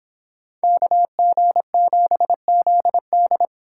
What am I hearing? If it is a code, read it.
Morse code, heard as a keyed tone, about 30 wpm, 700 Hz. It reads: KG7ZD